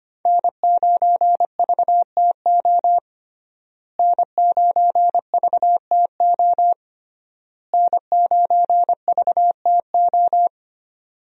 Morse code, heard as a keyed tone, about 25 wpm, 700 Hz. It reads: N9VTO N9VTO N9VTO
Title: USA Call Signs - 5 Characters - Repeated 3x - 25wpm